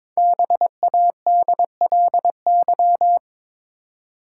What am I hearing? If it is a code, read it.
Morse code, heard as a keyed tone, about 22 wpm, 700 Hz. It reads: BADLY
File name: 5 Letter Words - First 3 Letters Given - 22wpm